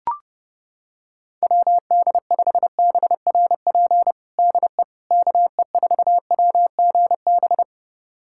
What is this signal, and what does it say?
Morse code, heard as a keyed tone, about 32 wpm, 700 Hz. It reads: WD5BRP DE KE4WGB
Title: USA Callsign DE Callsign 30wpm